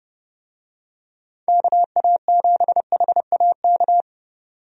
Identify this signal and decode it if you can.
Morse code, heard as a keyed tone, about 30 wpm, 700 Hz. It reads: KA7HAK
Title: USA Call Signs - 6 Characters - 30wpm